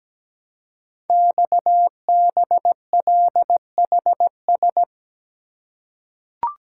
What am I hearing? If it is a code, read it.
Morse code, heard as a keyed tone, about 17 wpm, 700 Hz. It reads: XBLHS
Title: Code Groups - 5 Letters - 17wpm